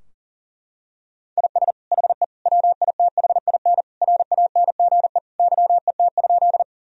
Morse code, heard as a keyed tone, about 40 wpm, 700 Hz. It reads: IS HE WITHIN RANGE YET?
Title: Sentences from Top 1000 Words - Review Part 4 - 40wpm